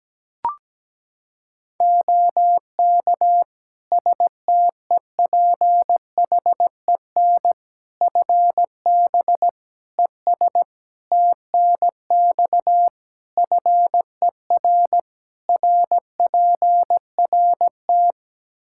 Morse code, heard as a keyed tone, about 17 wpm, 700 Hz. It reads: OK STEPHEN FB ES TNX FER RPRT
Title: K1USN SST QSO Practice 17wpm